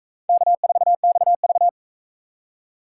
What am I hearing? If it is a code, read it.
Morse code, heard as a keyed tone, about 37 wpm, 700 Hz. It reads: K4XV